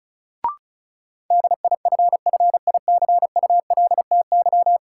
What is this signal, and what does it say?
Morse code, heard as a keyed tone, about 36 wpm, 700 Hz. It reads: DIFFICULTY
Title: Sentences from Top 1500 Words - New Word Drill Alt - 35wpm